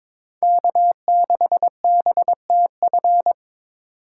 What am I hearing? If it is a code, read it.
Morse code, heard as a keyed tone, about 26 wpm, 700 Hz. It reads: K6BTF
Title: Speed Racing - USA 1x3 Call Signs 22wpm